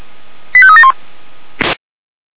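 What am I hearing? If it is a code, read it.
Five-tone selective call (ZVEI), heard as quick runs of five tones, tones of about 70 ms. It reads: 85371
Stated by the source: ZVEI five-tone selective call 85381 received by SDR.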